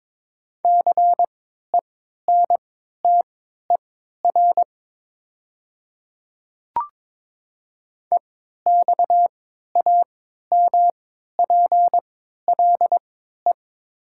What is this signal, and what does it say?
Morse code, heard as a keyed tone, about 22 wpm, spaced as for 12 wpm, 700 Hz. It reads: CENTER EXAMPLE
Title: 3x Character Spacing - Top 500 Words - 22wpm